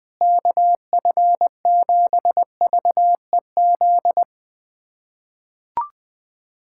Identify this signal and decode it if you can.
Morse code, heard as a keyed tone, about 20 wpm, 690 Hz. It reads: KF7VEZ
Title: USA Callsign DE Callsign 20wpm